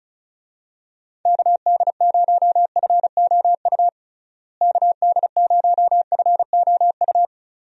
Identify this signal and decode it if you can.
Morse code, heard as a keyed tone, about 36 wpm, 700 Hz. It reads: KD0FOU KD0FOU
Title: USA Call Signs - 6 Characters - Repeated 2x - 35wpm